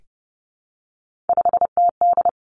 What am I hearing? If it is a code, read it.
Morse code, heard as a keyed tone, about 30 wpm, 700 Hz. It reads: 5TD
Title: Three Challenging Letter-Number Combinations 30wpm